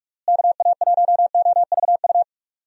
Morse code, heard as a keyed tone, about 41 wpm, 700 Hz. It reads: KA1OVU
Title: USA Call Signs - 6 Characters - Repeated 3x - 45wpm